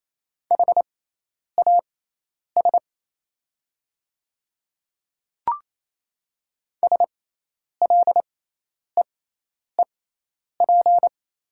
Morse code, heard as a keyed tone, about 27 wpm, spaced as for 10 wpm, 690 Hz. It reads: HAS SLEEP
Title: 6x Character Spacing - Top 500 Words - 28wpm